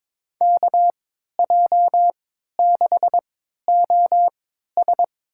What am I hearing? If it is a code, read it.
Morse code, heard as a keyed tone, about 22 wpm, spaced as for 12 wpm, 700 Hz. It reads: KJ6OS
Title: USA Call Signs - 5 Characters - 3x Character Spacing - 22wpm